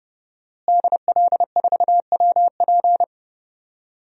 Morse code, heard as a keyed tone, about 30 wpm, 700 Hz. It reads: DL4WP
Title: DXCC-100 Call Signs - 5 Characters - Repeated 2x - 30wpm